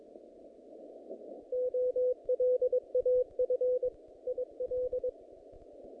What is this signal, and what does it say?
Morse code, heard as a keyed tone, about 22 wpm, 500 Hz. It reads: OLAF IL
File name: contests / naqp-cw-k5tr-op-k5pi- -2200